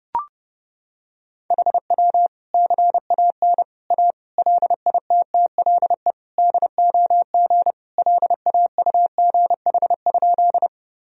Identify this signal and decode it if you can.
Morse code, heard as a keyed tone, about 30 wpm, 700 Hz. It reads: HW CAN A LITTLE DOG LAUGH?
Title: Sentences from Top 500 Words and Abbreviations 30wpm